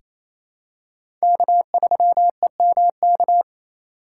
Morse code, heard as a keyed tone, about 28 wpm, 700 Hz. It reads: K3EMK
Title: USA Call Signs - 5 Characters - 2x Character Spacing - 28wpm